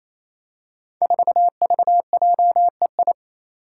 Morse code, heard as a keyed tone, about 27 wpm, 700 Hz. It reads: 4VJEI
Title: Five Challenging Letter-Number Combinations 28wpm